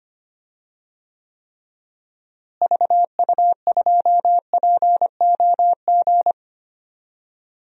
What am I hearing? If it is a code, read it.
Morse code, heard as a keyed tone, about 25 wpm, 700 Hz. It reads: VU2POG
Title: DXCC-100 Call Signs - 6 Characters - Repeated 3x - 25wpm